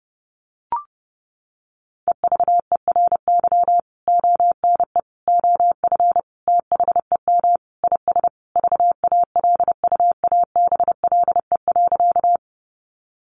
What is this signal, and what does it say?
Morse code, heard as a keyed tone, about 30 wpm, 700 Hz. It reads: EVERY ONE OF THEM IS VALUABLE.
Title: Sentences from Top 1100 Words 30wpm